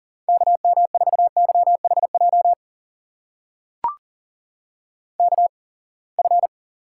Morse code, heard as a keyed tone, about 39 wpm, spaced as for 27 wpm, 700 Hz. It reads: KM4YHJ KF
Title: USA Call Signs - 6 Characters - 8x Character Spacing - 40wpm